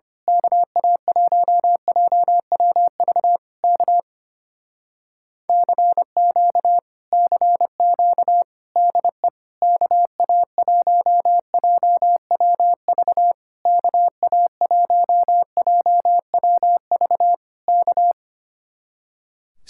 Morse code, heard as a keyed tone, about 24 wpm, 700 Hz. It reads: KA1JWV K CQ CQ DE KA1JWV KA1JWV K
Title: Speed-Racing Common QSO Phrases - 25wpm